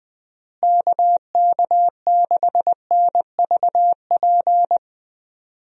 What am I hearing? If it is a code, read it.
Morse code, heard as a keyed tone, about 20 wpm, 700 Hz. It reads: KK6NVP
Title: USA Call Signs - 6 Characters - Repeated 3x - 20wpm